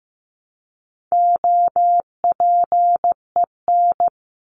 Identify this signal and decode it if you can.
Morse code, heard as a keyed tone, about 15 wpm, 700 Hz. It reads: OPEN
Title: Top 300 Words 15wpm